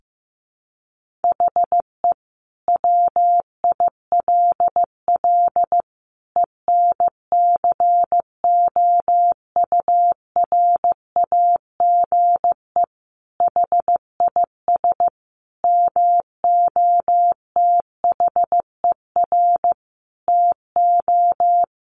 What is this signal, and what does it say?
Morse code, heard as a keyed tone, about 15 wpm, 700 Hz. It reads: HE WILL ENCOURAGE HIS MOTHER TO
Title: Sentences from Top 1400 Words 15wpm